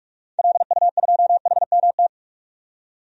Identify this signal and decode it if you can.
Morse code, heard as a keyed tone, about 46 wpm, 700 Hz. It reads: LU2HMT
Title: DXCC-100 Call Signs - 6 Characters - 45wpm